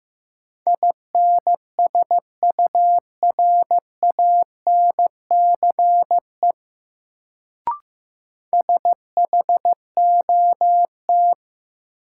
Morse code, heard as a keyed tone, about 15 wpm, 700 Hz. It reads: INSURANCE SHOT
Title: Sentences from Top 1100 Words - New Word Drill - 15wpm